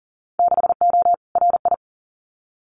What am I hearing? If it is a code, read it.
Morse code, heard as a keyed tone, about 41 wpm, 700 Hz. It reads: 6O RI
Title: Winter Field Day Exchanges Encore 40wpm